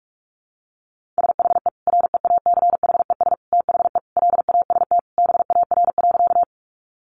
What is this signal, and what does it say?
Morse code, heard as a keyed tone, about 44 wpm, 700 Hz. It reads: SHE REACHES THE LAST BAR.
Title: Easy Phrases 45wpm